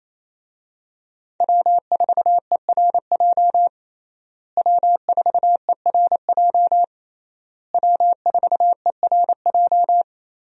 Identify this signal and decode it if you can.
Morse code, heard as a keyed tone, about 28 wpm, 700 Hz. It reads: W4ERJ W4ERJ W4ERJ
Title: USA Call Signs - 5 Characters - Repeated 3x - 28wpm